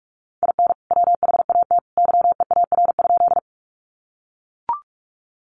Morse code, heard as a keyed tone, about 44 wpm, 700 Hz. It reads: IN WHAT YEAR?
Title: Sentences from Top 200 Words 45wpm